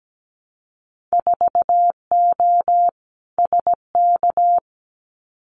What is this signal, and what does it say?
Morse code, heard as a keyed tone, about 17 wpm, 700 Hz. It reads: 4O SK
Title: Winter Field Day Exchanges Encore 17wpm